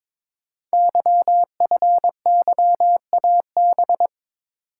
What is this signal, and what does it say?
Morse code, heard as a keyed tone, about 22 wpm, 700 Hz. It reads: YFYAB
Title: Code Groups - 5 Letter:Numbers - 22wpm